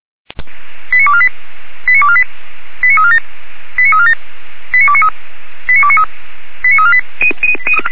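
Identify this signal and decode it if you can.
Five-tone selective call (ZVEI), heard as quick runs of five tones, tones of about 70 ms. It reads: 89248 89248 89347 89347 89293 89293 89348
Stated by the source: ZVEI five-tone call received by SDR.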